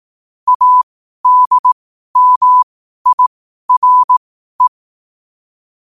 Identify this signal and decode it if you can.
Morse code, heard as a keyed tone, about 18 wpm, spaced as for 12 wpm, 1000 Hz. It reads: ADMIRE
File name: qotdfiles / 12wpm